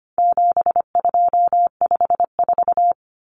Morse code, heard as a keyed tone, about 25 wpm, 700 Hz. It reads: 7254
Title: Common Cut Numbers - 4-Digit 25wpm